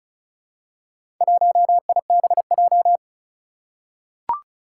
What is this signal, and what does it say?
Morse code, heard as a keyed tone, about 36 wpm, 700 Hz. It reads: 1IBJ
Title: Four Challenging Letter-Number Combinations 35wpm